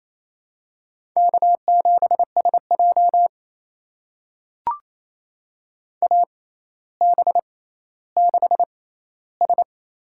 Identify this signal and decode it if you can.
Morse code, heard as a keyed tone, about 28 wpm, spaced as for 14 wpm, 700 Hz. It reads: K7SJ AB6S
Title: USA Call Signs - 4 Characters - 6x Character Spacing - 28wpm